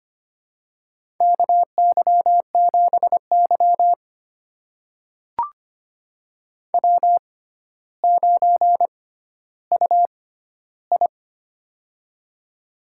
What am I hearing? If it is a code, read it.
Morse code, heard as a keyed tone, about 25 wpm, spaced as for 13 wpm, 700 Hz. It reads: KY7Y W9UI
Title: USA Call Signs - 4 Characters - 6x Character Spacing - 25wpm